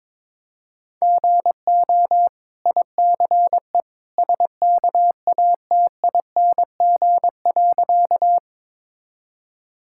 Morse code, heard as a keyed tone, about 23 wpm, 690 Hz. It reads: GO ICE SKATING.
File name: practice / 22wpm / Easy Sentences with Hints - 22wpm